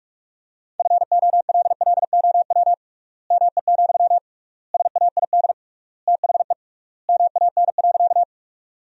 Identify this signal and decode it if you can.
Morse code, heard as a keyed tone, about 42 wpm, 700 Hz. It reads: FOLLOW ME, SAID THE MAN.
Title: 3x Word-Spacing - Sentences from Top 200 Words - 45wpm